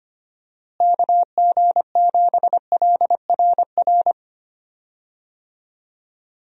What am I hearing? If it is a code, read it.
Morse code, heard as a keyed tone, about 25 wpm, 700 Hz. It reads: KG7LRR
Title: USA Call Signs - 6 Characters - Repeated 3x - 25wpm